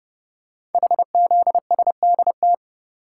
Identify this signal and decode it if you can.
Morse code, heard as a keyed tone, about 30 wpm, 700 Hz. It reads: HZSDT
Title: Code Groups - 5 Letters - 30wpm